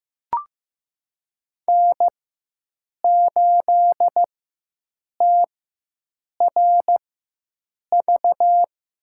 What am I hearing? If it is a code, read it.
Morse code, heard as a keyed tone, about 15 wpm, spaced as for 7 wpm, 700 Hz. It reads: N8TRV
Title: USA Call Signs - 4x Character Spacing - 15wpm